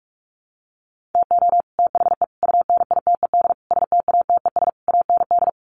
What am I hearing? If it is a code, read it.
Morse code, heard as a keyed tone, about 44 wpm, 700 Hz. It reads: TO THE UNITED STATES AND
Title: Sets of 5 Words Encore 45wpm